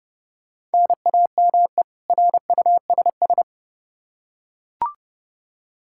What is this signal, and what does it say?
Morse code, heard as a keyed tone, about 32 wpm, 700 Hz. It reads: NAME RUSS
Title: K1USN SST QSO Practice 30wpm